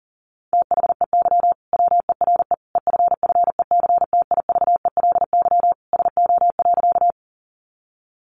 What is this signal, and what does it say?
Morse code, heard as a keyed tone, about 40 wpm, 700 Hz. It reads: THEY WERE EFFECTIVELY SO.
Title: Sentences from Top 1400 Words 40wpm